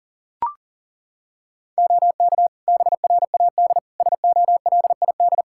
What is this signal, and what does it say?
Morse code, heard as a keyed tone, about 40 wpm, 700 Hz. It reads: OK BRAD SOLID
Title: Practice QSO - Exchange 3 - More Info - 40wpm